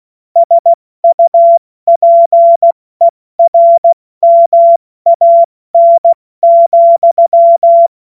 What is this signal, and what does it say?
Morse code, heard as a keyed tone, about 16 wpm, 680 Hz. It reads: SUPERMAN,